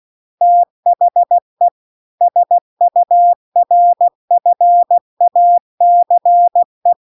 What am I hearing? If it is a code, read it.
Morse code, heard as a keyed tone, about 16 wpm, 700 Hz. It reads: THE SURFACE